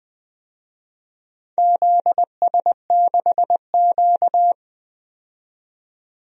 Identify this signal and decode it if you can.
Morse code, heard as a keyed tone, about 20 wpm, 700 Hz. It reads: ZS6Q
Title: DXCC-100 Call Signs - 4 Characters - 20wpm